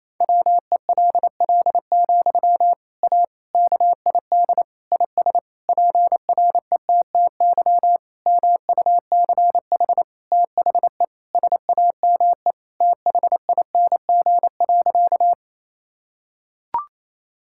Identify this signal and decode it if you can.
Morse code, heard as a keyed tone, about 28 wpm, 700 Hz. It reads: WELL, A KID IS PRETTY MUCH THE SAME THING.
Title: Sentences from Top 1600 Words 28wpm